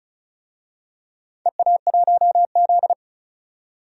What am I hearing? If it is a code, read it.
Morse code, heard as a keyed tone, about 34 wpm, 700 Hz. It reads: EA1Z
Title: DXCC-100 Call Signs - 4 Characters - 35wpm